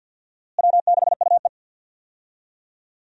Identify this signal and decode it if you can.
Morse code, heard as a keyed tone, about 52 wpm, 700 Hz. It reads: W6UE
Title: USA Call Signs - 4 Characters - Repeated 3x - 50wpm